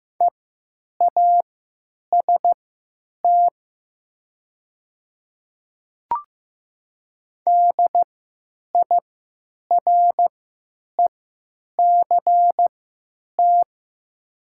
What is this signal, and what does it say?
Morse code, heard as a keyed tone, about 15 wpm, spaced as for 9 wpm, 700 Hz. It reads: EAST DIRECT
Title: 3x Character Spacing - Top 500 Words - 15wpm